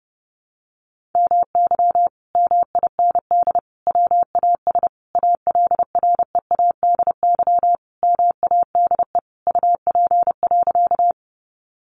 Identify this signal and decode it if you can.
Morse code, heard as a keyed tone, about 30 wpm, 700 Hz. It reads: MY MIND WAS ALREADY MADE UP.
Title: Sentences from Top 700 Words 30wpm